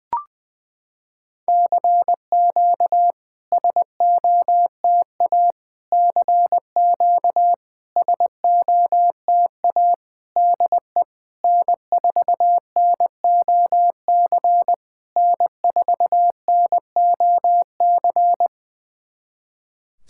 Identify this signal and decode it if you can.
Morse code, heard as a keyed tone, about 20 wpm, 690 Hz. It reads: CQ SOTA CQ SOTA DE N4NOC N4NOC
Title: SOTA QSO Practice 20wpm